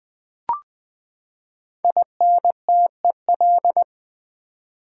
Morse code, heard as a keyed tone, about 20 wpm, 700 Hz. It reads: INTEL
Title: Build A Word - 5 Letters At A Time - 12-Letter Words - 20wpm